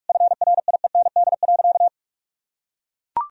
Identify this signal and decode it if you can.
Morse code, heard as a keyed tone, about 42 wpm, 700 Hz. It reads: FRIEND.
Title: Sentences from Top 1000 Words - Review Part 4 - 45wpm